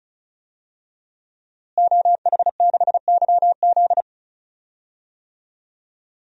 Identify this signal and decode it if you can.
Morse code, heard as a keyed tone, about 36 wpm, 700 Hz. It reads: OH6YZ
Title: DXCC-100 Call Signs - 5 Characters - Repeated 2x - 35wpm